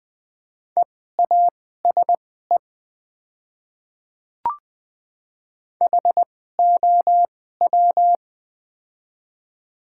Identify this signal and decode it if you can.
Morse code, heard as a keyed tone, about 20 wpm, spaced as for 14 wpm, 700 Hz. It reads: EASE HOW
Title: 2x Character Spacing - Top 300 Words - 20wpm